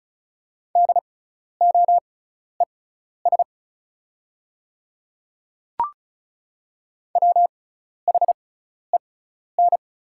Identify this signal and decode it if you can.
Morse code, heard as a keyed tone, about 36 wpm, spaced as for 12 wpm, 700 Hz. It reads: DOES WHEN
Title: 6x Character Spacing - Top 200 Words - 35wpm